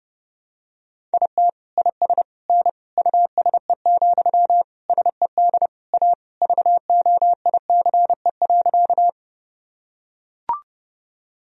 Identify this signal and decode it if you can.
Morse code, heard as a keyed tone, about 30 wpm, 700 Hz. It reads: IT IS N USE, SED A VOICE.
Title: Sentences from Top 500 Words and Abbreviations 30wpm